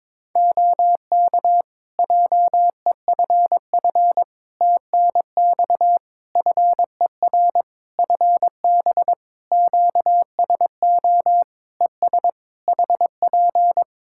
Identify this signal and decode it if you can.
Morse code, heard as a keyed tone, about 22 wpm, 700 Hz. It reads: OK JEFF TNX FER FB QSO ES HP
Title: K1USN SST QSO Practice 22wpm